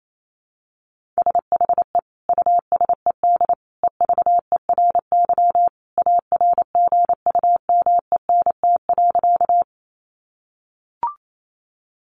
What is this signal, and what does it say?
Morse code, heard as a keyed tone, about 28 wpm, 700 Hz. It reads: SHE USED EVERY ARGUMENT.
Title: Sentences from Top 1400 Words 28wpm